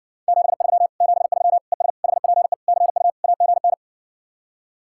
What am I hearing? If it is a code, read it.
Morse code, heard as a keyed tone, about 60 wpm, 700 Hz. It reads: 73 73 ES HPE CU AGN